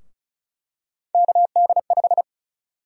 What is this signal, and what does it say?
Morse code, heard as a keyed tone, about 34 wpm, 700 Hz. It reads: KD5
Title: USA Call Sign Prefixes - Repeated 2x - 35wpm